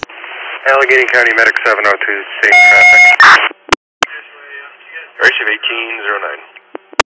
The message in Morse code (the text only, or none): none